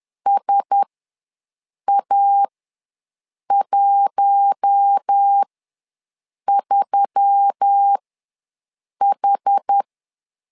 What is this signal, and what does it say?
Morse code, heard as a keyed tone, about 10 wpm, spaced as for 6 wpm, 800 Hz. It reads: SA13H
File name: LEZIONI / 4 Q1S3 ws 10wpm 2437